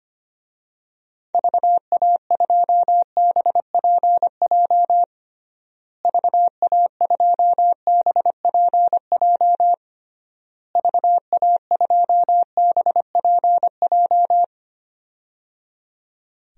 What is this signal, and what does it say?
Morse code, heard as a keyed tone, about 25 wpm, 700 Hz. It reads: VA2BPJ VA2BPJ VA2BPJ
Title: DXCC-100 Call Signs - 6 Characters - Repeated 3x - 25wpm